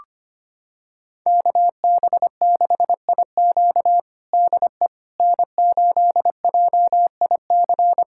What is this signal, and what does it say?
Morse code, heard as a keyed tone, about 25 wpm, 700 Hz. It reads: KB6IQ DE N8JIC
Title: USA Callsign DE Callsign - 5 Characters - 25wpm